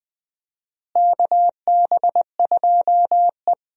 Morse code, heard as a keyed tone, about 20 wpm, 700 Hz. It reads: KB2E
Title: USA Call Signs - 4 Characters - Repeated 3x - 20wpm